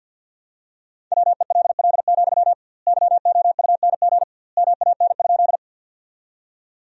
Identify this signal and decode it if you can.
Morse code, heard as a keyed tone, about 50 wpm, 700 Hz. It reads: WELL, YOUNG MAN?
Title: 2x Word-Spacing - Sentences from Top 400 Words - 50wpm